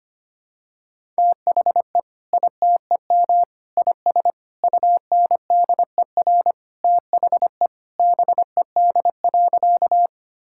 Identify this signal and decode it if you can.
Morse code, heard as a keyed tone, about 25 wpm, 700 Hz. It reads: THE ITEM IS UNDER THE BED.
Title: Sentences from Top 1000 Words - Review Part 6 - 25wpm